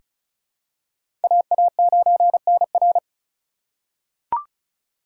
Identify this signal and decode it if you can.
Morse code, heard as a keyed tone, about 36 wpm, 700 Hz. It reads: AA9NR
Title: USA Call Signs - 5 Characters - 3x Character Spacing - 35wpm